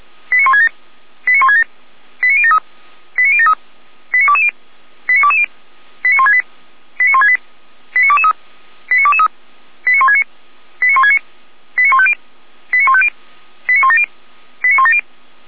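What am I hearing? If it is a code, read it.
Five-tone selective call (ZVEI), heard as quick runs of five tones, tones of about 70 ms. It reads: 89167 89167 89073 89073 89220 89220 89168 89168 89223 89223 89169 89169 89150 89150 89170 89170
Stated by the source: Five-tone ZVEI selective call 89022 received on an SDR, fire brigade Orlamünde (feuerwehr-orlamuende).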